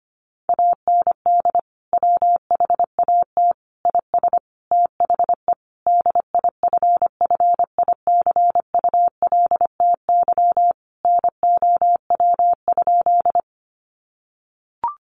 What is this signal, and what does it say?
Morse code, heard as a keyed tone, about 25 wpm, 700 Hz. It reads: AND WHAT IS THE DIFFICULTY NOW?
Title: Sentences from Top 1500 Words 25wpm